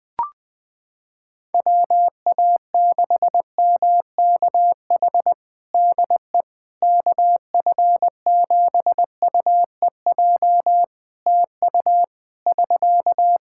Morse code, heard as a keyed tone, about 20 wpm, 700 Hz. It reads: WA6MKH DE KF7UEJ TU <SK>
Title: K1USN SST QSO Practice 20wpm